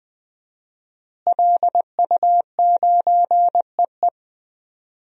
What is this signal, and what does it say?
Morse code, heard as a keyed tone, about 20 wpm, 700 Hz. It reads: LU9EE
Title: DXCC-100 Call Signs - 5 Characters - Repeated 2x - 20wpm